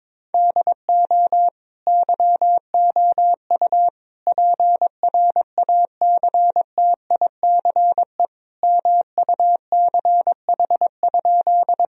Morse code, heard as a keyed tone, about 22 wpm, 700 Hz. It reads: DO YOU PRACTICE MUCH?
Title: Sentences from Top 1000 Words - Review Part 4 - 22wpm